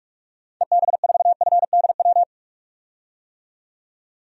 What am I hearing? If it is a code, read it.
Morse code, heard as a keyed tone, about 46 wpm, 700 Hz. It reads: EB4FDW
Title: DXCC-100 Call Signs - 6 Characters - 45wpm